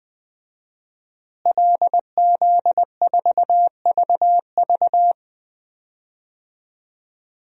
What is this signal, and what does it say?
Morse code, heard as a keyed tone, about 20 wpm, 700 Hz. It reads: LZ4VV
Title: DXCC-100 Call Signs - 5 Characters - Repeated 2x - 20wpm